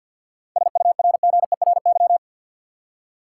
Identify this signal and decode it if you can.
Morse code, heard as a keyed tone, about 53 wpm, 700 Hz. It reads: SURGERY
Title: Seven Letter Words - 50wpm